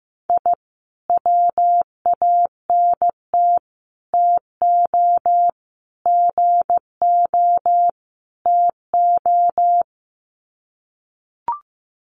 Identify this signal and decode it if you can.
Morse code, heard as a keyed tone, about 15 wpm, 700 Hz. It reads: I WANT TO GO TO
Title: Sets of 5 Words Encore 15wpm